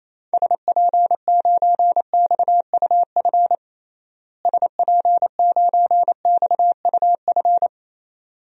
Morse code, heard as a keyed tone, about 28 wpm, 700 Hz. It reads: SP9XUF SP9XUF
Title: DXCC-100 Call Signs - 6 Characters - Repeated 2x - 28wpm